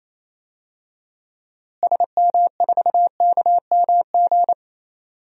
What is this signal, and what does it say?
Morse code, heard as a keyed tone, about 28 wpm, 700 Hz. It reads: SM4KMG
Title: DXCC-100 Call Signs - 6 Characters - Repeated 3x - 28wpm